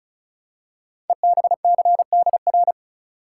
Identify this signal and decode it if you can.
Morse code, heard as a keyed tone, about 36 wpm, 690 Hz. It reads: EBCDR
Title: Code Groups - 5 Letters - 35wpm